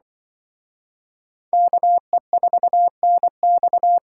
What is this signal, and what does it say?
Morse code, heard as a keyed tone, about 25 wpm, 700 Hz. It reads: KE4NX
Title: Speed Racing - USA Call Signs 20wpm